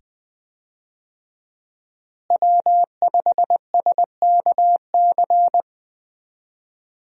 Morse code, heard as a keyed tone, about 20 wpm, 700 Hz. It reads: W5SKC
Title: USA Call Signs - 5 Characters - Repeated 2x - 20wpm